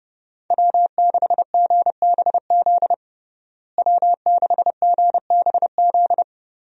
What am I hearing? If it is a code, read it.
Morse code, heard as a keyed tone, about 30 wpm, 700 Hz. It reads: W6GBZ W6GBZ
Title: USA Call Signs - 5 Characters - Repeated 2x - 30wpm